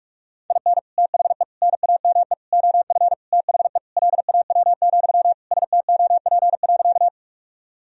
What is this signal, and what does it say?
Morse code, heard as a keyed tone, about 43 wpm, 700 Hz. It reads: IN THE NAME OF THE LAW, STOP.
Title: Sentences from Top 1000 Words - Review Part 3 - 45wpm